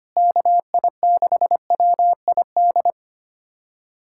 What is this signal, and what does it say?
Morse code, heard as a keyed tone, about 25 wpm, 700 Hz. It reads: KI6WID
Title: USA Callsign DE Callsign - 6 Characters - 25wpm